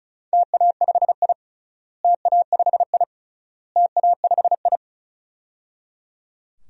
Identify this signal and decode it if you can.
Morse code, heard as a keyed tone, about 36 wpm, 700 Hz. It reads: TA5I TA5I TA5I
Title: DXCC-100 Call Signs - 4 Characters - Repeated 3x - 35wpm